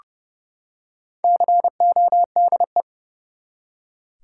Morse code, heard as a keyed tone, about 32 wpm, 700 Hz. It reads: CODE
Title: Four Letter Words - 30wpm